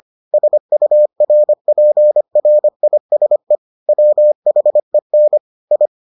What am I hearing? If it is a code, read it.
Morse code, heard as a keyed tone, about 25 wpm, 600 Hz. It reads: SURPRISE WHEN I